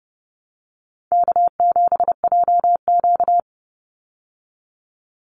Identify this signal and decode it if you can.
Morse code, heard as a keyed tone, about 30 wpm, 700 Hz. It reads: K7JQ
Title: Easy Call Signs Repeated 3x - 30wpm